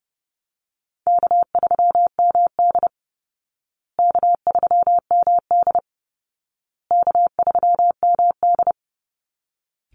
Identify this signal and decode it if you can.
Morse code, heard as a keyed tone, about 30 wpm, 700 Hz. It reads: K3MD K3MD K3MD
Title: Easy Call Signs Repeated 3x - 30wpm